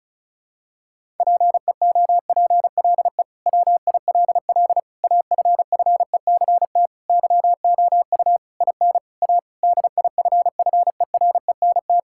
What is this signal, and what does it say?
Morse code, heard as a keyed tone, about 36 wpm, 700 Hz. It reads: PEOPLE WILL AFFECT YOU IN A DIFFERENT
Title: Sentences from Top 1000 Words - Review Part 6 - 35wpm